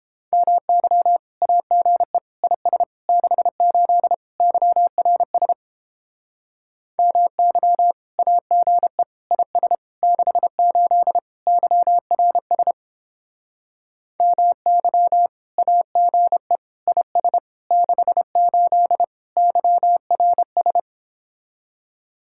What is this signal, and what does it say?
Morse code, heard as a keyed tone, about 29 wpm, 700 Hz. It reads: MY AGE IS 68 YRS MY AGE IS 68 YRS MY AGE IS 68 YRS